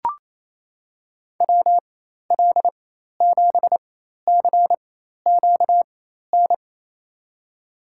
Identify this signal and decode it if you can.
Morse code, heard as a keyed tone, about 28 wpm, spaced as for 13 wpm, 700 Hz. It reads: WL7CQN
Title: USA Call Signs - 6 Characters - 4x Character Spacing - 28wpm